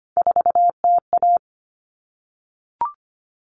Morse code, heard as a keyed tone, about 25 wpm, 700 Hz. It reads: 4TA
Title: All Cut Numbers - 3-Digit 25wpm